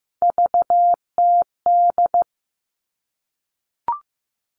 Morse code, heard as a keyed tone, about 15 wpm, 700 Hz. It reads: VTD
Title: All Cut Numbers - 3-Digit 15wpm